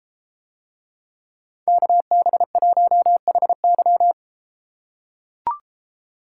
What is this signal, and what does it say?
Morse code, heard as a keyed tone, about 33 wpm, 700 Hz. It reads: KB1HY